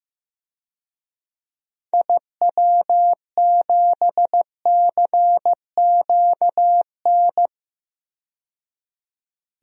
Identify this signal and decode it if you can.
Morse code, heard as a keyed tone, about 15 wpm, 700 Hz. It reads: IW7CQN
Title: DXCC-100 Call Signs - 6 Characters - 15wpm